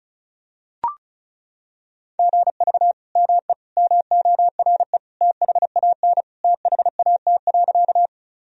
Morse code, heard as a keyed tone, about 34 wpm, 700 Hz. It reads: GV ME MORE THAN THAT.
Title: Sentences from Top 200 Words and Abbreviations 35wpm